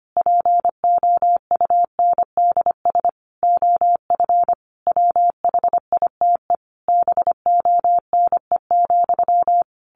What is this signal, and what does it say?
Morse code, heard as a keyed tone, about 25 wpm, 700 Hz. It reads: POUNDS OF WHITE BONE,
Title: A Sound Of Thunder 25wpm